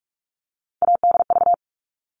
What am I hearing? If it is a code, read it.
Morse code, heard as a keyed tone, about 44 wpm, 700 Hz. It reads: ADV
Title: All Cut Numbers - 3-Digit 45wpm